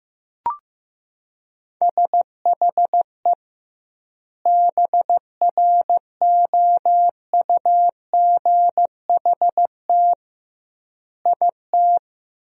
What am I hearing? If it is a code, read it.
Morse code, heard as a keyed tone, about 15 wpm, 700 Hz. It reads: SHE BROUGHT IT
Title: 2x Word-Spacing - Sentences from Top 500 Words - 15wpm